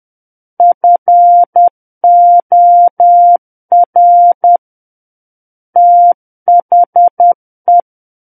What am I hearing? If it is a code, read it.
Morse code, heard as a keyed tone, about 10 wpm, 700 Hz. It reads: FOR THE
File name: Book One / 01-Chapter 1 - 10 WPM